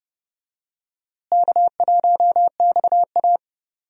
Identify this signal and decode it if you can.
Morse code, heard as a keyed tone, about 32 wpm, 700 Hz. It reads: K1XA
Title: USA Call Signs - 4 Characters - Repeated 2x - 30wpm